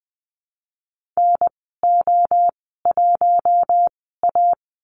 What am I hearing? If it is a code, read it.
Morse code, heard as a keyed tone, about 20 wpm, spaced as for 14 wpm, 700 Hz. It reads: NO1A
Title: 042 Lesson 8g - Number 1 - Call Signs 20wpm